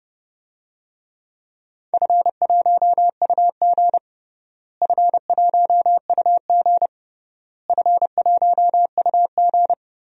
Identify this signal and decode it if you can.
Morse code, heard as a keyed tone, about 30 wpm, 700 Hz. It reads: F1UG F1UG F1UG
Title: DXCC-100 Call Signs - 4 Characters - Repeated 3x - 30wpm